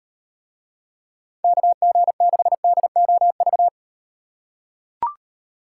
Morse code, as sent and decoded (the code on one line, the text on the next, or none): -.- --. -.... -.. --- ...-
KG6DOV